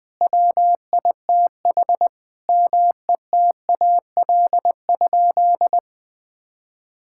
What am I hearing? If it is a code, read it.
Morse code, heard as a keyed tone, about 20 wpm, 700 Hz. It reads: WITH METAL?
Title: Sentences from Top 1600 Words 20wpm